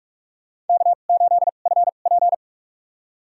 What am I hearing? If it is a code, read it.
Morse code, heard as a keyed tone, about 44 wpm, spaced as for 32 wpm, 700 Hz. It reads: K8FP
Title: USA Call Signs - 2x Character Spacing - 45wpm